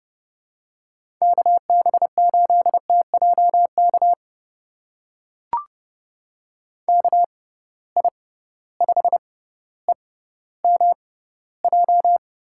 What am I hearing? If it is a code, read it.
Morse code, heard as a keyed tone, about 32 wpm, spaced as for 16 wpm, 700 Hz. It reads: KB8TJK KI5EMJ
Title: USA Call Signs - 6 Characters - 6x Character Spacing - 30wpm